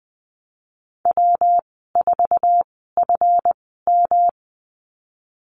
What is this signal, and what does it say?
Morse code, heard as a keyed tone, about 20 wpm, spaced as for 15 wpm, 700 Hz. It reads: W4FM
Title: 200 Lesson 32g - Letter k - Call Signs 20wpm